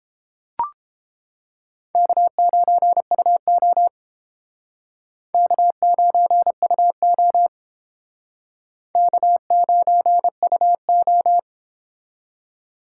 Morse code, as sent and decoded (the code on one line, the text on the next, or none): -.- ----. ..- --- / -.- ----. ..- --- / -.- ----. ..- ---
K9UO K9UO K9UO